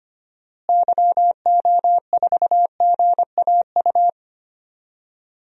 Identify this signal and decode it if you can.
Morse code, heard as a keyed tone, about 25 wpm, 690 Hz. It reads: YO4GAU
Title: DXCC-100 Call Signs - 6 Characters - Repeated 3x - 25wpm